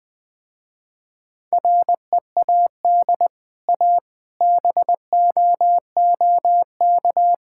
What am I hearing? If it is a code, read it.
Morse code, heard as a keyed tone, about 20 wpm, 700 Hz. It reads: READ A BOOK
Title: Easy Sentences with Hints - 20wpm